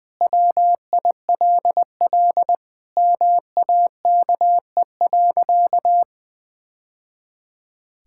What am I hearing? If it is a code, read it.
Morse code, heard as a keyed tone, about 21 wpm, 700 Hz. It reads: WILL MAKE.
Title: Sentences from Top 1000 Words - Review Part 5 - 20wpm